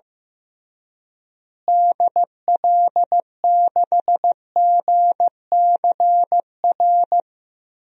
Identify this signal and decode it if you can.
Morse code, heard as a keyed tone, about 16 wpm, 700 Hz. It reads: DL6GCR
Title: DXCC-100 Call Signs - 6 Characters - Repeated 3x - 15wpm